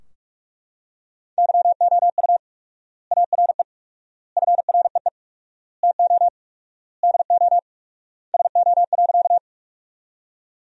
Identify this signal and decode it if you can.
Morse code, heard as a keyed tone, about 41 wpm, 700 Hz. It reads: YOU ARE FREE TO DO SO.
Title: 4x Word-Spacing - Sentences from Top 500 Words - 45wpm